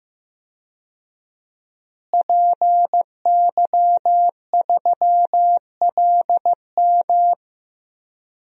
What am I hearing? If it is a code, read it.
Morse code, heard as a keyed tone, about 15 wpm, 700 Hz. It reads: PY3LM
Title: DXCC-100 Call Signs - 5 Characters - Repeated 3x - 15wpm